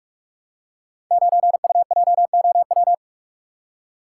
Morse code, heard as a keyed tone, about 44 wpm, 700 Hz. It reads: 9UJOW
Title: Code Groups - 5 Letter:Numbers - 45wpm